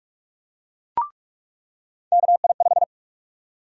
Morse code, heard as a keyed tone, about 48 wpm, 700 Hz. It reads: KI5
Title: USA Call Sign Prefixes - 45wpm